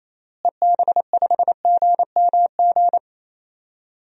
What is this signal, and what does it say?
Morse code, heard as a keyed tone, about 28 wpm, 700 Hz. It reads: EB5GMG
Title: DXCC-100 Call Signs - 6 Characters - Repeated 3x - 28wpm